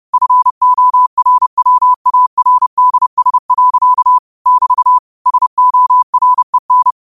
Morse code, heard as a keyed tone, about 30 wpm, 1000 Hz. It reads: FORWARDS. <BT> SOREN
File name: qotdfiles / 30wpm